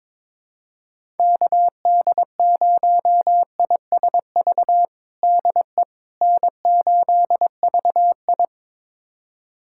Morse code, heard as a keyed tone, about 21 wpm, 700 Hz. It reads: KD0ISV DE N8VI
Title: USA Callsign DE Callsign 22wpm